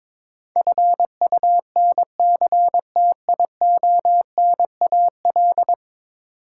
Morse code, heard as a keyed tone, about 22 wpm, 700 Hz. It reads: FUNCTIONAL
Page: morse-code-ninja-practice-22wpm